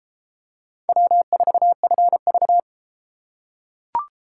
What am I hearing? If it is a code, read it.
Morse code, heard as a keyed tone, about 30 wpm, 700 Hz. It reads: W4FV